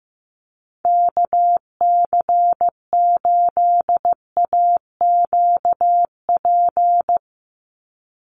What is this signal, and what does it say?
Morse code, heard as a keyed tone, about 15 wpm, 700 Hz. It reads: KC8AQP
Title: USA Call Signs 15wpm